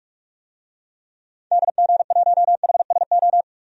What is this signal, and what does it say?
Morse code, heard as a keyed tone, about 42 wpm, 700 Hz. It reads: DG1HSO
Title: DXCC-100 Call Signs - 6 Characters - Repeated 2x - 45wpm